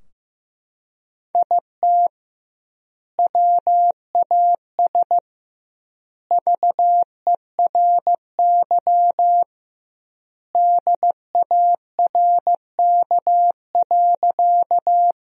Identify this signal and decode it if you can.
Morse code, heard as a keyed tone, about 15 wpm, 700 Hz. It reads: IT WAS VERY DARK.
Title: 2x Word-Spacing - Sentences from Top 500 Words - 15wpm